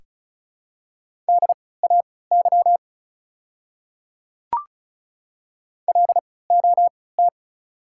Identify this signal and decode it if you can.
Morse code, heard as a keyed tone, about 36 wpm, spaced as for 20 wpm, 700 Hz. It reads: DAY LOT
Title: 3x Character Spacing - Top 500 Words - 35wpm